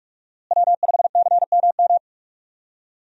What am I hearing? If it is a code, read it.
Morse code, heard as a keyed tone, about 46 wpm, 700 Hz. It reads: W5CMM
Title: USA Call Signs - 5 Characters - Repeated 2x - 45wpm